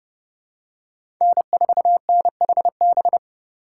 Morse code, heard as a keyed tone, about 30 wpm, 700 Hz. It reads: N4NHB
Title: USA Call Signs - 5 Characters - 6x Character Spacing - 30wpm